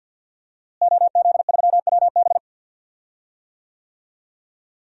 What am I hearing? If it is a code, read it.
Morse code, heard as a keyed tone, about 50 wpm, 700 Hz. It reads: OZ3WB
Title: DXCC-100 Call Signs - 5 Characters - Repeated 3x - 50wpm